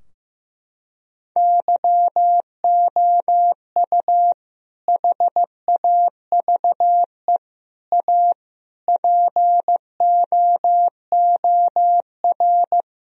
Morse code, heard as a keyed tone, about 14 wpm, 700 Hz. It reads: YOU HAVE A POOR
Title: Sentences from Top 1000 Words - Review Part 3 - 15wpm